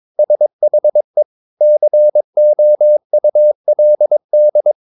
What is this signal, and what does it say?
Morse code, heard as a keyed tone, about 22 wpm, 600 Hz. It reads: SHE COULD
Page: Romanen Never af Ken Follet på engelsk.